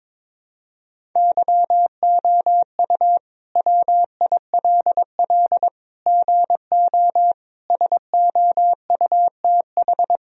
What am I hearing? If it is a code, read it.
Morse code, heard as a keyed tone, about 21 wpm, 700 Hz. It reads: YOU WILL GO SOUTH
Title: Sentences from Top 1000 Words - Review Part 2 - 22wpm